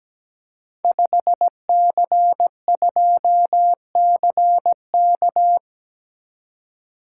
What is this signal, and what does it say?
Morse code, heard as a keyed tone, about 17 wpm, 700 Hz. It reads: 5C2CK
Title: Code Groups - 5 Letter:Numbers - 17wpm